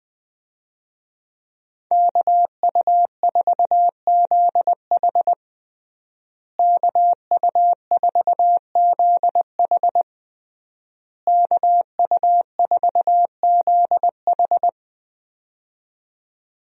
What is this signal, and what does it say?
Morse code, heard as a keyed tone, about 20 wpm, 700 Hz. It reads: KU4ZH KU4ZH KU4ZH
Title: USA Call Signs - 5 Characters - Repeated 3x - 20wpm